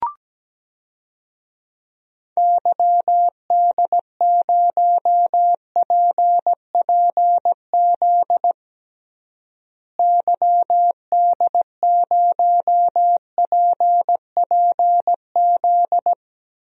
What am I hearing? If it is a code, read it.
Morse code, heard as a keyed tone, about 17 wpm, 700 Hz. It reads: YD0PPZ YD0PPZ
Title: DXCC-100 Call Signs - 6 Characters - Repeated 3x - 17wpm